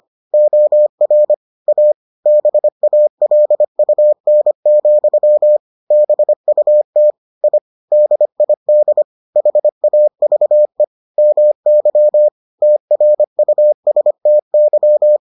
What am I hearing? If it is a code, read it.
Morse code, heard as a keyed tone, about 25 wpm, 600 Hz. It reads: OR A BALUN, BUT I DID HAVE MY TRUSTY